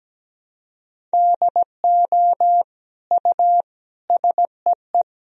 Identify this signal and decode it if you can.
Morse code, heard as a keyed tone, about 17 wpm, 700 Hz. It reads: DO U SEE
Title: Sentences from Top 100 Words and Abbreviations 17wpm